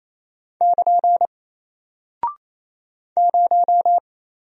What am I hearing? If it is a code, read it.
Morse code, heard as a keyed tone, about 27 wpm, 700 Hz. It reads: <KN> 0
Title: Single Character (Letter:Number:Punctuation:Prosign) Alt 28wpm